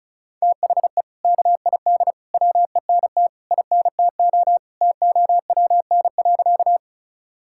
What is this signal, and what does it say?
Morse code, heard as a keyed tone, about 34 wpm, 700 Hz. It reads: THE KID WENT INTO TOWN.
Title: Sentences from Top 1600 Words 35wpm